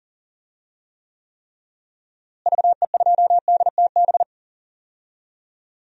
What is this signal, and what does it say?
Morse code, heard as a keyed tone, about 40 wpm, 700 Hz. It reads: VE2DTB
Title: DXCC-100 Call Signs - 6 Characters - Repeated 3x - 40wpm